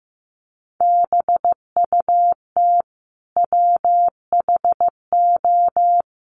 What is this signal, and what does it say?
Morse code, heard as a keyed tone, about 15 wpm, 700 Hz. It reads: BUT WHO
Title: Sentences from Top 100 Words 15wpm